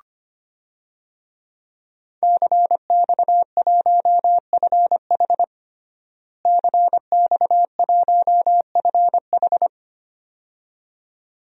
Morse code, heard as a keyed tone, about 25 wpm, 700 Hz. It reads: CX1FH CX1FH
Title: DXCC-100 Call Signs - 5 Characters - Repeated 2x - 25wpm